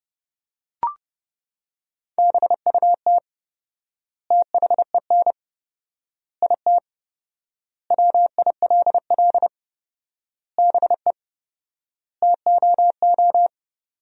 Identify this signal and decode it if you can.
Morse code, heard as a keyed tone, about 31 wpm, 700 Hz. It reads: BUT THEN IT WILL BE TOO
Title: 4x Word-Spacing - Sentences from Top 300 Words - 30wpm